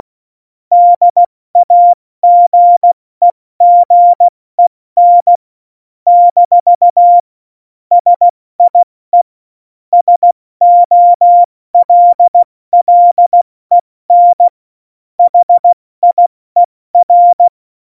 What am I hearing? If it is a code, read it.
Morse code, heard as a keyed tone, about 16 wpm, spaced as for 14 wpm, 700 Hz. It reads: DAGEGEN - SIE SOLLEN HIER
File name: Morsekurs / Witze / Witz 21. Januar 2020, 70 F80